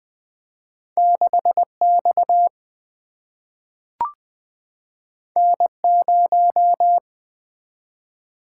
Morse code, heard as a keyed tone, about 20 wpm, 700 Hz. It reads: 6X N0
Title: Single-Letter Single-Number 20wpm